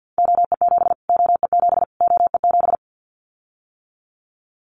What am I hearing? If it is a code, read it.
Morse code, heard as a keyed tone, about 50 wpm, 700 Hz. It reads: OE7 OE7 OE7
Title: Callsign Prefixes for Top 25 DXCC - Repeated 3x - 50wpm